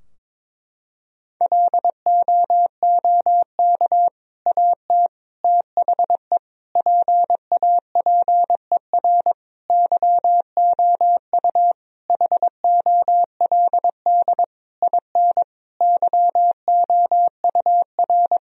Morse code, heard as a keyed tone, about 21 wpm, 700 Hz. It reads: LOOK AT THE PAPER YOU HOLD IN YOUR
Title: Sentences from Top 1000 Words - Review Part 2 - 22wpm